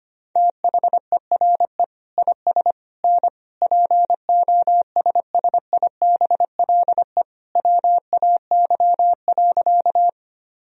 Morse code, heard as a keyed tone, about 25 wpm, 700 Hz. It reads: THERE IS N POSSIBLE WAY.
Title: Sentences from Top 500 Words and Abbreviations 25wpm